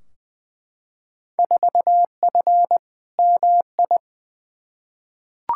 Morse code, heard as a keyed tone, about 20 wpm, 700 Hz. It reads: 4F MI
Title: Field Day Exchanges - 20wpm